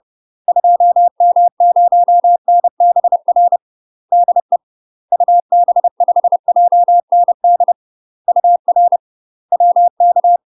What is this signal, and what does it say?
Morse code, heard as a keyed tone, about 30 wpm, 700 Hz. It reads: 2M0NBR DE UB5JND UR WK
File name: UK MASTER QSOs TRAINING / 30WPM / UK MASTER QSOs 0009